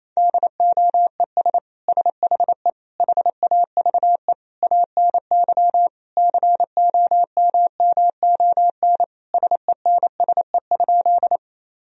Morse code, heard as a keyed tone, about 28 wpm, 700 Hz. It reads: DOES SHE HAVE ANY COMMON SENSE?
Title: Sentences from Top 1000 Words - Review Part 4 - 28wpm